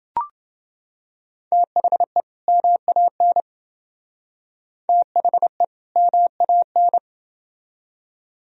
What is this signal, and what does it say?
Morse code, heard as a keyed tone, about 27 wpm, 700 Hz. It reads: THE MAN THE MAN